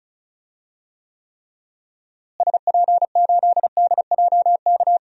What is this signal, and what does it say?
Morse code, heard as a keyed tone, about 36 wpm, 700 Hz. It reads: SP8DJK
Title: DXCC-100 Call Signs - 6 Characters - Repeated 3x - 35wpm